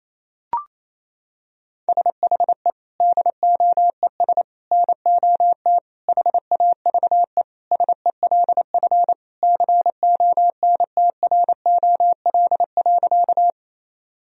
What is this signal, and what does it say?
Morse code, heard as a keyed tone, about 28 wpm, 700 Hz. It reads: SHE DOES NOT HAVE SELF CONTROL.
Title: Sentences from Top 1000 Words - Review Part 3 - 28wpm